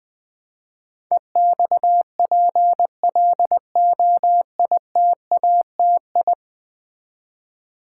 Morse code, heard as a keyed tone, about 20 wpm, 700 Hz. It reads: EXPLOITATI
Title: Build A Word - 5 Letters At A Time - 12-Letter Words - 20wpm